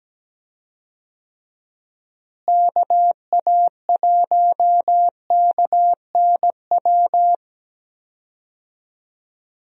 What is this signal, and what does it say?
Morse code, heard as a keyed tone, about 17 wpm, 700 Hz. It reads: KA1KNW
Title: USA Call Signs - 6 Characters - Repeated 2x - 17wpm